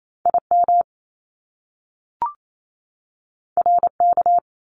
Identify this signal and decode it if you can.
Morse code, heard as a keyed tone, about 28 wpm, 700 Hz. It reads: IM RK
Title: Top Two Letter Combinations in Words 28wpm